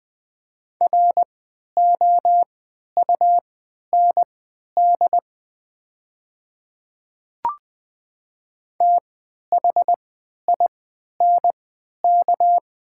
Morse code, heard as a keyed tone, about 20 wpm, spaced as for 11 wpm, 700 Hz. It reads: ROUND THINK
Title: 3x Character Spacing - Top 200 Words - 20wpm